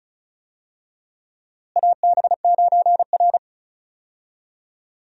Morse code, heard as a keyed tone, about 36 wpm, 700 Hz. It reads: AB9R
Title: USA Call Signs - 4 Characters - Repeated 3x - 35wpm